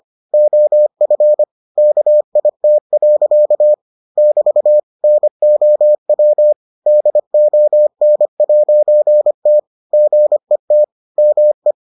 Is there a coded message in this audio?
Yes: Morse code from 0.3 to 11.7 s, heard as a keyed tone, about 25 wpm, 600 Hz. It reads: OF KIT. <BT> NOW DON'T GET ME